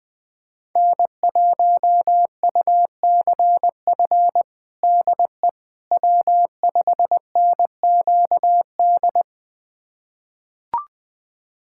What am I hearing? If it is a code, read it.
Morse code, heard as a keyed tone, about 20 wpm, 700 Hz. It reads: N1UCF DE W5NQD
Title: USA Callsign DE Callsign - 5 Characters - 20wpm